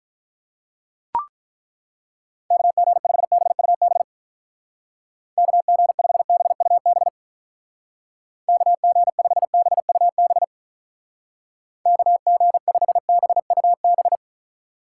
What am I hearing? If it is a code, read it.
Morse code, heard as a keyed tone, about 43 wpm, 700 Hz. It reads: KG5BUB KG5BUB KG5BUB KG5BUB
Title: Speed Racing - USA Call Signs 35wpm